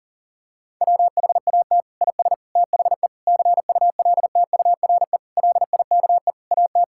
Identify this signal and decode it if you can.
Morse code, heard as a keyed tone, about 37 wpm, 690 Hz. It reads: WHAT IS THE CULTURE LIKE AT
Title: Sentences from Top 1000 Words - Review Part 6 - 40wpm